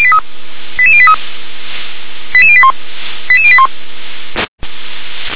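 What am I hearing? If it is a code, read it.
Five-tone selective call (ZVEI), heard as quick runs of five tones, tones of about 70 ms. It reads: 88083 88081 88081
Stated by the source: ZVEI five-tone selective call received on an SDR.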